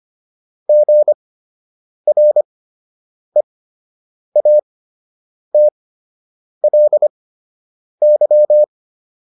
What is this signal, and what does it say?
Morse code, heard as a keyed tone, about 25 wpm, spaced as for 8 wpm, 600 Hz. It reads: GREATLY